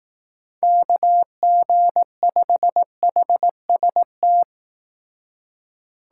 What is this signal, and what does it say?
Morse code, heard as a keyed tone, about 18 wpm, 700 Hz. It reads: KG5HST
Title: Speed Racing - USA Call Signs 15wpm